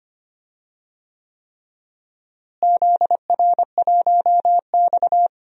Morse code, heard as a keyed tone, about 25 wpm, 700 Hz. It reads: ZR1X
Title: DXCC-100 Call Signs - 4 Characters - Repeated 3x - 25wpm